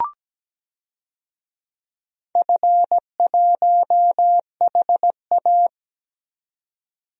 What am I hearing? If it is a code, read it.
Morse code, heard as a keyed tone, about 17 wpm, 700 Hz. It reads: F1HA
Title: DXCC-100 Call Signs - 4 Characters - 17wpm